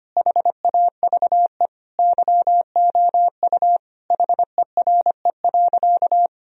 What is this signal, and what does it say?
Morse code, heard as a keyed tone, about 25 wpm, 700 Hz. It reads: HAVE YOU HERE.
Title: Sentences from Top 1000 Words - Review Part 2 - 25wpm